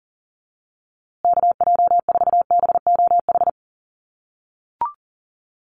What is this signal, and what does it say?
Morse code, heard as a keyed tone, about 40 wpm, 700 Hz. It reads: KJ4BOH
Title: USA Call Signs Repeated 2x 40wpm